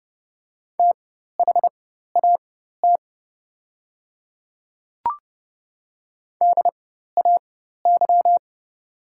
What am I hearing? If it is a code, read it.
Morse code, heard as a keyed tone, about 30 wpm, spaced as for 14 wpm, 690 Hz. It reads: THAT DAY
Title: 4x Character Spacing - Top 100 Words - 30wpm